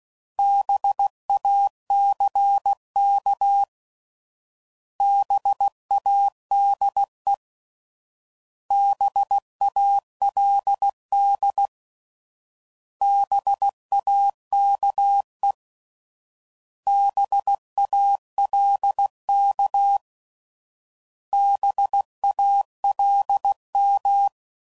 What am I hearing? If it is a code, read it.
Morse code, heard as a keyed tone, about 16 wpm, 800 Hz. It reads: BACK BADE BALD BAKE BALK BALM